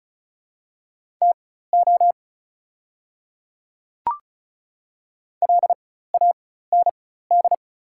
Morse code, heard as a keyed tone, about 36 wpm, spaced as for 16 wpm, 690 Hz. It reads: TO LAND
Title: 4x Character Spacing - Top 200 Words - 35wpm